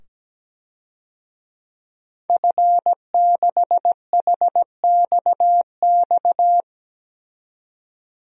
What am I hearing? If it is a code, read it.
Morse code, heard as a keyed tone, about 17 wpm, 700 Hz. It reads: F6HXX
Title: DXCC-100 Call Signs - 5 Characters - 17wpm